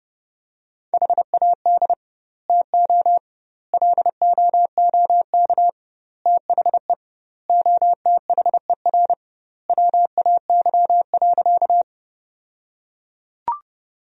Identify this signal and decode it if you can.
Morse code, heard as a keyed tone, about 32 wpm, 700 Hz. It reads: HAD TO LOOK THE OTHER WAY.
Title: 2x Word-Spacing - Sentences from Top 100 Words - 30wpm